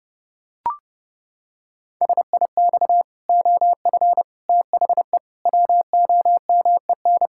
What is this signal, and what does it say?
Morse code, heard as a keyed tone, about 30 wpm, 700 Hz. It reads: SIX OF THE WOMEN
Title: Pangram Challenge v2 - 30wpm